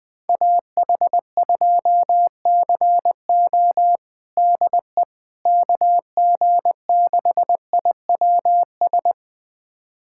Morse code, heard as a keyed tone, about 20 wpm, 700 Hz. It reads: AH2CO DE KG6IWS
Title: USA Callsign DE Callsign 20wpm